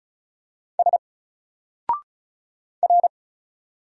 Morse code, heard as a keyed tone, about 36 wpm, 700 Hz. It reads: S R
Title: Single Character (Letter:Number:Punctuation:Prosign) 35wpm